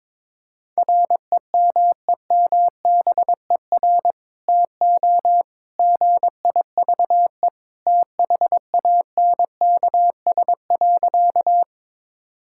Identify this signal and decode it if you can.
Morse code, heard as a keyed tone, about 22 wpm, 700 Hz. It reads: REMEMBER TO GIVE THANKS.
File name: practice / 22wpm / Sentences with 2-Word Phrases - 22wpm